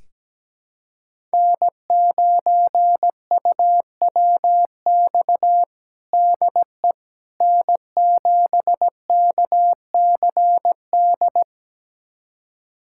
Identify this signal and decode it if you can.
Morse code, heard as a keyed tone, about 17 wpm, 700 Hz. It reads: N9UWX DE N7KCD
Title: USA Callsign DE Callsign - 5 Characters - 17wpm